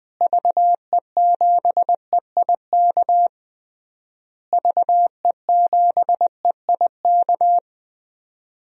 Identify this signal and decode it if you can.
Morse code, heard as a keyed tone, about 20 wpm, 700 Hz. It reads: VE7EIK VE7EIK
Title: DXCC-100 Call Signs - 6 Characters - Repeated 3x - 20wpm